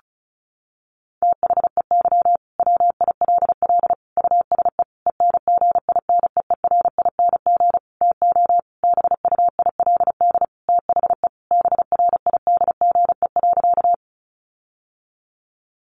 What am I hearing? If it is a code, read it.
Morse code, heard as a keyed tone, about 34 wpm, 700 Hz. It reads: THEY WILL USE ENGINEERING TO BUILD THE BRIDGE.
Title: Sentences from Top 1500 Words 35wpm